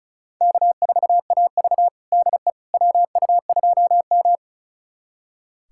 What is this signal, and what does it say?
Morse code, heard as a keyed tone, about 36 wpm, 700 Hz. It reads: K4AV DE WU2M
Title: USA Callsign DE Callsign - 4 Characters - 35wpm